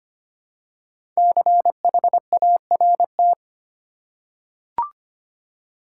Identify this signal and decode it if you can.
Morse code, heard as a keyed tone, about 25 wpm, 700 Hz. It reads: CHART
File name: 5 Letter Words - First 3 Letters Given - 25wpm